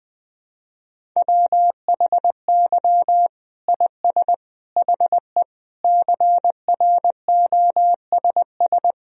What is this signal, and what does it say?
Morse code, heard as a keyed tone, about 20 wpm, 700 Hz. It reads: WHY IS HE CROSS
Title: Sentences from Top 1000 Words - Review Part 1 - 20wpm